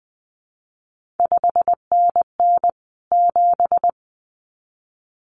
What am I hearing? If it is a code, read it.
Morse code, heard as a keyed tone, about 20 wpm, 700 Hz. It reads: 5NN 7
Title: CQ WW DX Contest Exchange Encore - 20wpm